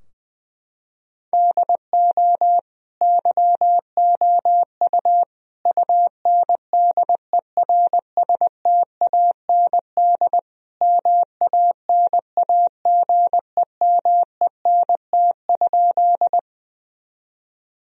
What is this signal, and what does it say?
Morse code, heard as a keyed tone, about 20 wpm, 700 Hz. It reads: DO YOU UNDERSTAND MANAGEMENT?
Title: Sentences from Top 1000 Words - Review Part 4 - 20wpm